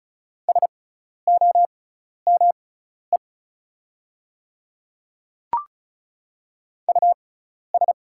Morse code, heard as a keyed tone, about 34 wpm, spaced as for 12 wpm, 700 Hz. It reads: SOME US
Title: 6x Character Spacing - Top 200 Words - 35wpm